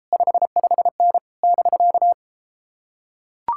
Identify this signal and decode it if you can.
Morse code, heard as a keyed tone, about 33 wpm, 700 Hz. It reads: 55N <BK>